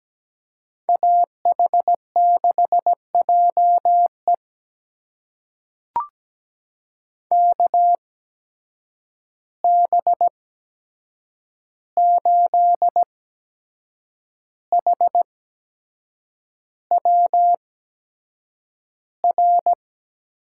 Morse code, heard as a keyed tone, about 17 wpm, spaced as for 7 wpm, 700 Hz. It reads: AH6JE KB8HWR
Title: USA Call Signs - 8x Character Spacing - 17wpm